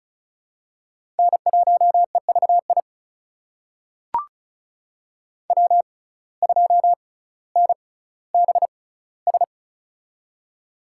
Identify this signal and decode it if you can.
Morse code, heard as a keyed tone, about 36 wpm, spaced as for 18 wpm, 700 Hz. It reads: N1EVI W2NBS
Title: USA Call Signs - 5 Characters - 6x Character Spacing - 35wpm